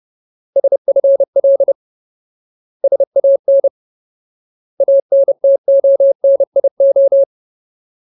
Morse code, heard as a keyed tone, about 30 wpm, 550 Hz. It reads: SFL SAN ANTONIO
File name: Ultimate Word List / 30WPM Ultimate